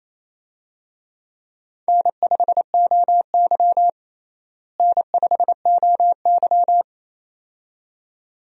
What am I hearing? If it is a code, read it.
Morse code, heard as a keyed tone, about 28 wpm, 700 Hz. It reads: N5OY N5OY
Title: USA Call Signs - 4 Characters - Repeated 2x - 28wpm